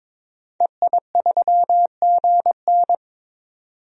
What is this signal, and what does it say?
Morse code, heard as a keyed tone, about 22 wpm, 700 Hz. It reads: EI3GN
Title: DXCC-100 Call Signs - 5 Characters - 22wpm